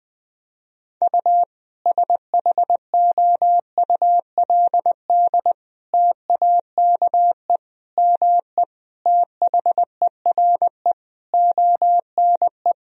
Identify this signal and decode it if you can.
Morse code, heard as a keyed tone, about 21 wpm, 700 Hz. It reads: U SHOULD TAKE ME THERE ONE
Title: Sentences from Top 300 Words and Abbreviations 20wpm